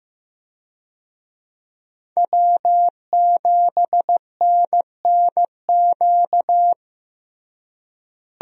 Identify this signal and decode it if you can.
Morse code, heard as a keyed tone, about 16 wpm, 700 Hz. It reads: W7NNQ
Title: USA Call Signs - 5 Characters - Repeated 2x - 15wpm